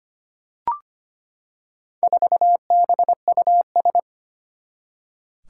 Morse code, heard as a keyed tone, about 25 wpm, 700 Hz. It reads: 4BUS
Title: Four Challenging Letter-Number Combinations 25wpm